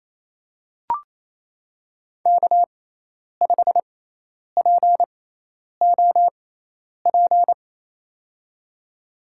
Morse code, heard as a keyed tone, about 27 wpm, spaced as for 10 wpm, 700 Hz. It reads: K5POP
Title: USA Call Signs - 6x Character Spacing - 28wpm